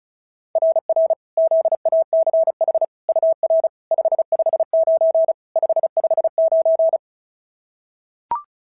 Morse code, heard as a keyed tone, about 31 wpm, 650 Hz. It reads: RR ZACH UR 559 559
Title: SOTA QSO Practice 35wpm